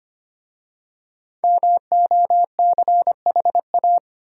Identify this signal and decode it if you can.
Morse code, heard as a keyed tone, about 25 wpm, 690 Hz. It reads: MOCHA
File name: practice / 25wpm / Easy Words with Hints - 25wpm